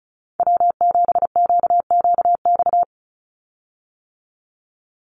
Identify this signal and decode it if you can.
Morse code, heard as a keyed tone, about 34 wpm, 700 Hz. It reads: W7QQX
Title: USA Call Signs 35wpm